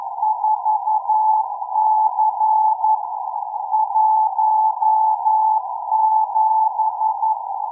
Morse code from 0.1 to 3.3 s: 4C